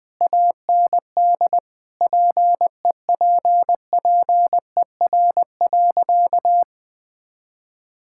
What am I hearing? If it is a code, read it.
Morse code, heard as a keyed tone, about 20 wpm, 700 Hz. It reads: AND PEPPER.
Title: Sentences with 5-Word Phrases - 20wpm